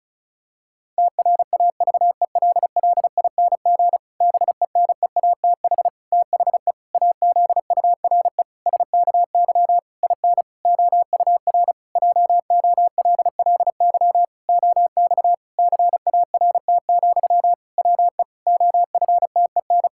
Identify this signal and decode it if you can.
Morse code, heard as a keyed tone, about 36 wpm, 700 Hz. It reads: TRAVELLING BENEATH THE AZURE SKY IN OUR JOLLY OX CART, WE OFTEN
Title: Pangram Challenge v2 - 35wpm